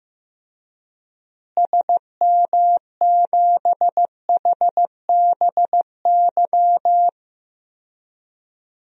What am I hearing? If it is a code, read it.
Morse code, heard as a keyed tone, about 15 wpm, 700 Hz. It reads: SM7HBY
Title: DXCC-100 Call Signs - 6 Characters - Repeated 3x - 15wpm